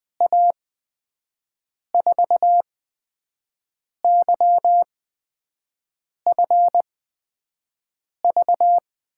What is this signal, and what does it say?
Morse code, heard as a keyed tone, about 20 wpm, spaced as for 5 wpm, 700 Hz. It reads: A4YFV
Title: USA Call Signs - 6 Characters - 8x Character Spacing - 20wpm